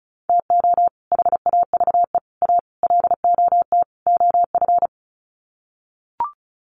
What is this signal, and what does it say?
Morse code, heard as a keyed tone, about 34 wpm, 700 Hz. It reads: TO HAVE A LOT OF
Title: Sets of 5 Words Encore 35wpm